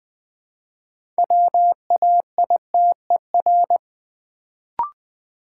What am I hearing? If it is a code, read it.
Morse code, heard as a keyed tone, about 20 wpm, 690 Hz. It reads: WAITER
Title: Build A Word - 3 Letters At A Time - 6-Letter Words - 20wpm